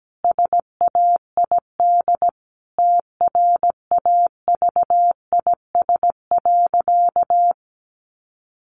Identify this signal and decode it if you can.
Morse code, heard as a keyed tone, about 17 wpm, 700 Hz. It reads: SAID TRAVIS.
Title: A Sound Of Thunder 17wpm